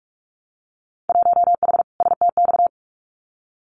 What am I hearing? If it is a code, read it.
Morse code, heard as a keyed tone, about 46 wpm, 700 Hz. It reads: 1H STX
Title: Winter Field Day Exchanges 45wpm